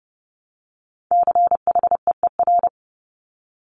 Morse code, heard as a keyed tone, about 32 wpm, 700 Hz. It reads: CHEER
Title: Multi-letter Phonemes Repeated 3x 30wpm